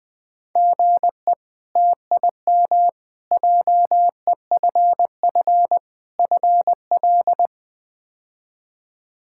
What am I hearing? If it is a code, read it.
Morse code, heard as a keyed tone, about 20 wpm, 700 Hz. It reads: GE TIM JEFF FL
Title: K1USN SST QSO Practice - 20wpm